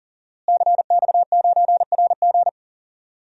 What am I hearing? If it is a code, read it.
Morse code, heard as a keyed tone, about 40 wpm, 700 Hz. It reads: CX9RG
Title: DXCC-100 Call Signs - 5 Characters - Repeated 2x - 40wpm